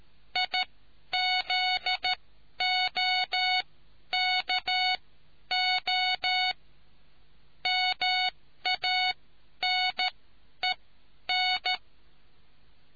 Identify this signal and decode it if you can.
Morse code, heard as a keyed tone, about 13 wpm, spaced as for 10 wpm, 740 Hz. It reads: IZOKO MANEN